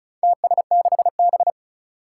Morse code, heard as a keyed tone, about 36 wpm, 700 Hz. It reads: TS6B
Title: Four Challenging Letter-Number Combinations 35wpm